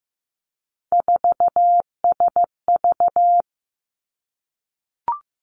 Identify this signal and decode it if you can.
Morse code, heard as a keyed tone, about 15 wpm, 700 Hz. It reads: 4SV
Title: Three Challenging Letter-Number Combinations 15wpm